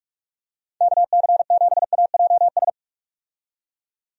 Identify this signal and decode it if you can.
Morse code, heard as a keyed tone, about 46 wpm, 700 Hz. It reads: KC7AJS
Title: USA Call Signs - 6 Characters - 45wpm